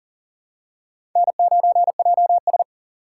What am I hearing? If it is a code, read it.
Morse code, heard as a keyed tone, about 40 wpm, 700 Hz. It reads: N9JS
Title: USA Call Signs - 6x Character Spacing - 40wpm